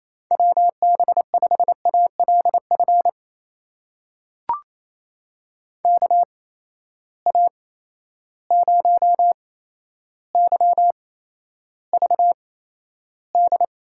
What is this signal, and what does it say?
Morse code, heard as a keyed tone, about 28 wpm, spaced as for 12 wpm, 700 Hz. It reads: WB5ALF KA0YVD
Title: USA Call Signs - 6 Characters - 8x Character Spacing - 28wpm